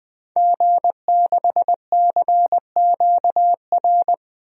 Morse code, heard as a keyed tone, about 20 wpm, 700 Hz. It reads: G6CQR
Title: DXCC-100 Call Signs - 5 Characters - 20wpm